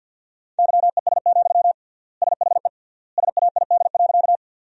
Morse code, heard as a keyed tone, about 50 wpm, 700 Hz. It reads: YES, SHE SAID.
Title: 3x Word-Spacing - Sentences from Top 500 Words - 50wpm